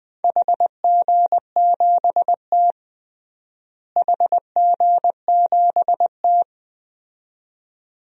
Morse code, heard as a keyed tone, about 20 wpm, 700 Hz. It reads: HG7T HG7T
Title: DXCC-100 Call Signs - 4 Characters - Repeated 2x - 20wpm